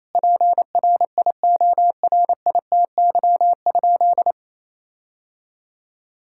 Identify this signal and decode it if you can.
Morse code, heard as a keyed tone, about 28 wpm, 700 Hz. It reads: PRIORITY?
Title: Sentences from Top 1600 Words 28wpm